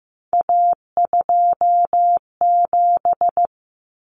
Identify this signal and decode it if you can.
Morse code, heard as a keyed tone, about 15 wpm, 700 Hz. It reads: A27
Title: Common Cut Numbers - 3-Digit 15wpm